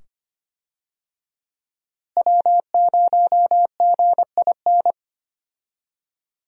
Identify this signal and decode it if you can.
Morse code, heard as a keyed tone, about 25 wpm, 700 Hz. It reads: W0GIN
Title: USA Call Signs - 5 Characters - Repeated 2x - 25wpm